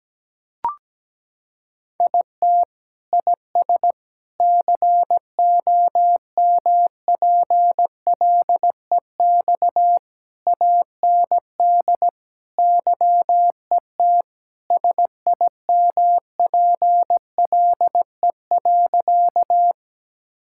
Morse code, heard as a keyed tone, about 17 wpm, 700 Hz. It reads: IT IS COMPLEX AND YET SIMPLE.
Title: Sentences from Top 1000 Words - Review Part 6 - 17wpm